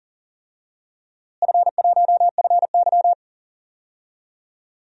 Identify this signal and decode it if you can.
Morse code, heard as a keyed tone, about 39 wpm, 700 Hz. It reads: F1FY
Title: DXCC-100 Call Signs - 4 Characters - Repeated 3x - 40wpm